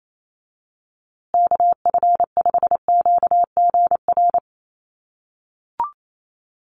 Morse code, heard as a keyed tone, about 28 wpm, 700 Hz. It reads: KF5QGR
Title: USA Call Signs Repeated 3x 28wpm